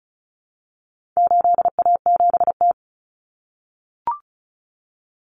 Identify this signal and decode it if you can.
Morse code, heard as a keyed tone, about 34 wpm, 700 Hz. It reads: 8A7T
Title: Common Cut Numbers - 4-Digit 35wpm